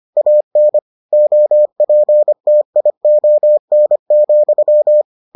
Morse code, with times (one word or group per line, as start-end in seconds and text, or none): 0.2-0.8 AN
1.1-5.0 OPTION,